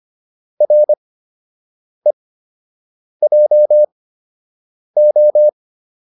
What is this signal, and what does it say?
Morse code, heard as a keyed tone, about 25 wpm, spaced as for 7 wpm, 600 Hz. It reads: REJO